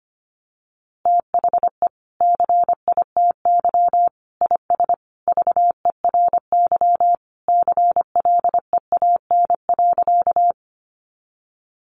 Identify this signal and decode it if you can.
Morse code, heard as a keyed tone, about 25 wpm, 700 Hz. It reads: THE CITY IS VERY CLEAN.
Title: Easy Phrases 25wpm